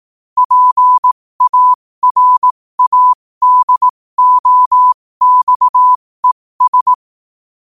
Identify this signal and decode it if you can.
Morse code, heard as a keyed tone, about 18 wpm, spaced as for 15 wpm, 1000 Hz. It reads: PARADOXES
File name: qotdfiles / 15wpm